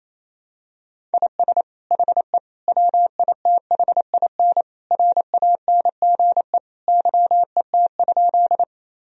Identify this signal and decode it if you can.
Morse code, heard as a keyed tone, about 28 wpm, 700 Hz. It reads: IS HE WITHIN RANGE YET?
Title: Sentences from Top 1000 Words - Review Part 4 - 28wpm